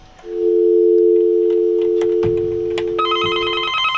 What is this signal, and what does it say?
Dial tone (North America), 350 + 440 Hz.